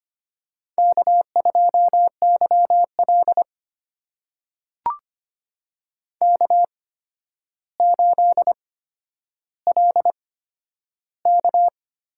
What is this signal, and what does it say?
Morse code, heard as a keyed tone, about 25 wpm, spaced as for 11 wpm, 700 Hz. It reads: K2YL K8LK
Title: USA Call Signs - 4 Characters - 8x Character Spacing - 25wpm